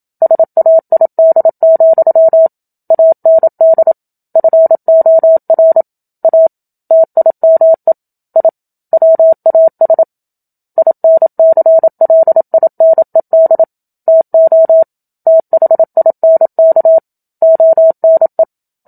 Morse code, heard as a keyed tone, about 27 wpm, 650 Hz. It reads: SAID, AND FOR A TIME I WAS INCLINED TO THINK ONE
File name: Book Two / 01-Chapter 1 - 27 WPM